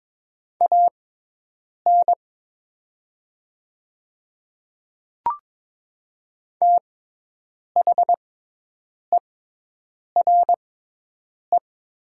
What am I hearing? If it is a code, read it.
Morse code, heard as a keyed tone, about 23 wpm, spaced as for 8 wpm, 700 Hz. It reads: AN THERE
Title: 6x Character Spacing - Top 200 Words - 22wpm